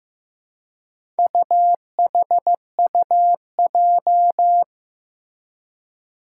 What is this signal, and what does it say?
Morse code, heard as a keyed tone, about 15 wpm, 700 Hz. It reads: UHUJ
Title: Four Challenging Letter-Number Combinations 15wpm